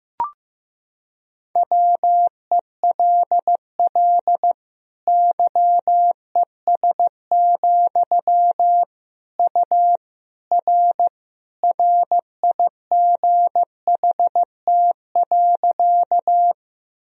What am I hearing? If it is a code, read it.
Morse code, heard as a keyed tone, about 16 wpm, 700 Hz. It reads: WELL YES, U R RIGHT.
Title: Sentences from Top 500 Words and Abbreviations 15wpm